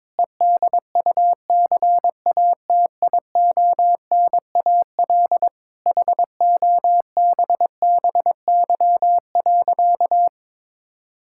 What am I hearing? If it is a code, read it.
Morse code, heard as a keyed tone, about 22 wpm, 700 Hz. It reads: EDUCATIONAL HOBBY.
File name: Difficult Sentences with Hints - 22wpm